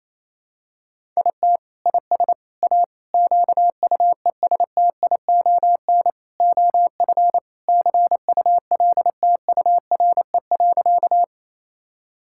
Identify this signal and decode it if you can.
Morse code, heard as a keyed tone, about 28 wpm, 700 Hz. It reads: IT IS A QUESTION OF CULTURE.
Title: Sentences from Top 1000 Words - Review Part 6 - 28wpm